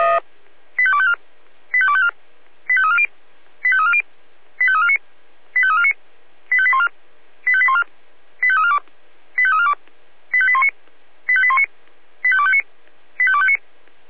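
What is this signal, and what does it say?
Five-tone selective call (ZVEI), heard as quick runs of five tones, tones of about 70 ms. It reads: 85254 85254 85340 85340 85349 85349 85814 85814 85342 85342 85819 85819 85259 85259
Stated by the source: ZVEI five-tone selective call received by SDR.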